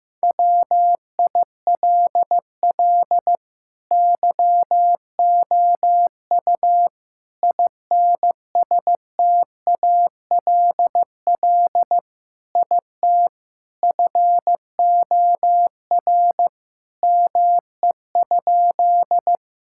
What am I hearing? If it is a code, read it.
Morse code, heard as a keyed tone, about 15 wpm, 700 Hz. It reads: WILL YOU INSTALL IT FOR ME?
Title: Sentences from Top 1600 Words 15wpm